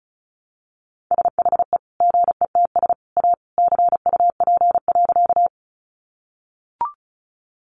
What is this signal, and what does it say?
Morse code, heard as a keyed tone, about 34 wpm, 700 Hz. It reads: SHE GETS A CUP.
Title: Additional Easy Phrases 35wpm